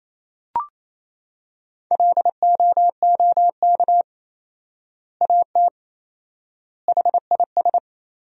Morse code, heard as a keyed tone, about 27 wpm, 700 Hz. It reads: LOOK AT HIS
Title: 4x Word-Spacing - Sentences from Top 300 Words - 28wpm